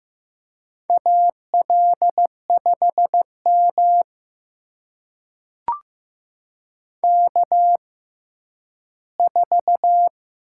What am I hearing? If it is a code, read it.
Morse code, heard as a keyed tone, about 15 wpm, spaced as for 10 wpm, 700 Hz. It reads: AL5M K4
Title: USA Call Signs - 6x Character Spacing - 15wpm